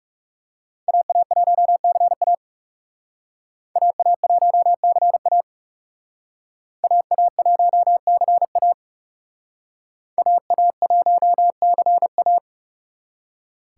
Morse code, heard as a keyed tone, about 38 wpm, 700 Hz. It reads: AA1CA AA1CA AA1CA AA1CA